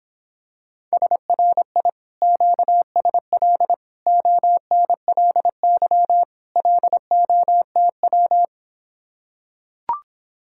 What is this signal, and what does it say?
Morse code, heard as a keyed tone, about 26 wpm, 700 Hz. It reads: SRI QSL ONLY LOTW